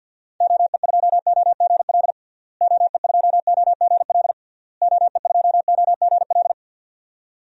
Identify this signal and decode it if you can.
Morse code, heard as a keyed tone, about 46 wpm, 700 Hz. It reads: OE2OGL OE2OGL OE2OGL
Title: DXCC-100 Call Signs - 6 Characters - Repeated 3x - 50wpm